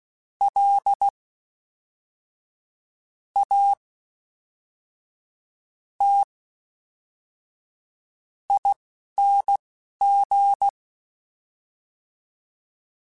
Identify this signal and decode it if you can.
Morse code, heard as a keyed tone, about 16 wpm, spaced as for 5 wpm, 800 Hz. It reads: LATING
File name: Special INGs / Special INGs-mp3s / Fast ing @10